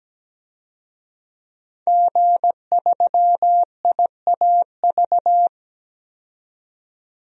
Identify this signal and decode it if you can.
Morse code, heard as a keyed tone, about 17 wpm, 700 Hz. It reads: G3IAV
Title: DXCC-100 Call Signs - 5 Characters - 17wpm